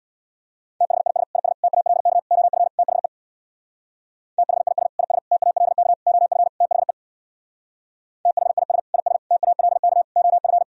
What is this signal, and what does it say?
Morse code, heard as a keyed tone, about 62 wpm, 700 Hz. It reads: THIS IS TALK OF THE THIS IS TALK OF THE THIS IS TALK OF
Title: Speed-Racing - Sets of 5 Words 50wpm